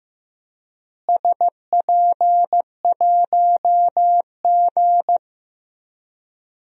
Morse code, heard as a keyed tone, about 15 wpm, 700 Hz. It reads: SP1G
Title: DXCC-100 Call Signs - 4 Characters - Repeated 2x - 15wpm